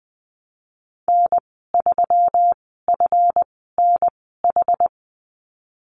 Morse code, heard as a keyed tone, about 20 wpm, spaced as for 14 wpm, 700 Hz. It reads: N3FNH
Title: 147 Lesson 24g - Letter y - Call Signs 20wpm